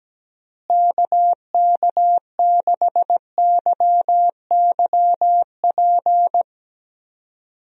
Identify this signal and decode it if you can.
Morse code, heard as a keyed tone, about 17 wpm, 700 Hz. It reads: KK6YYP
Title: USA Call Signs - 6 Characters - Repeated 2x - 17wpm